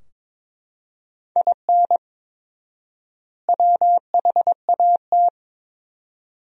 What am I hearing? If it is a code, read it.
Morse code, heard as a keyed tone, about 22 wpm, 700 Hz. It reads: IN WHAT
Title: 4x Word-Spacing - Sentences from Top 200 Words - 22wpm